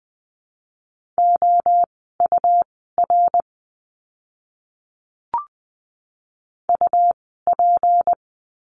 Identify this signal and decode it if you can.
Morse code, heard as a keyed tone, about 20 wpm, spaced as for 15 wpm, 700 Hz. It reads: OUR UP
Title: 206 Lesson 33f - Letter j - Words 20wpm